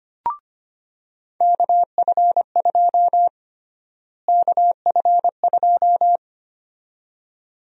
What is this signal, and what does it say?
Morse code, heard as a keyed tone, about 25 wpm, 700 Hz. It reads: KF2 KF2
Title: USA Call Sign Prefixes - Repeated 2x - 25wpm